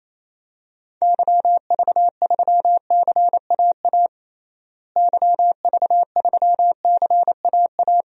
Morse code, heard as a keyed tone, about 28 wpm, 700 Hz. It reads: YV3CAA YV3CAA
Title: DXCC-100 Call Signs - 6 Characters - Repeated 2x - 28wpm